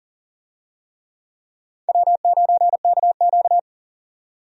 Morse code, heard as a keyed tone, about 40 wpm, 700 Hz. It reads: W9KQ
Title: USA Call Signs - 4 Characters - 40wpm